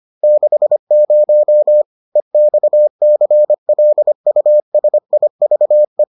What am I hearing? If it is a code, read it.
Morse code, heard as a keyed tone, about 25 wpm, 600 Hz. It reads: 60 EXCLUSIVE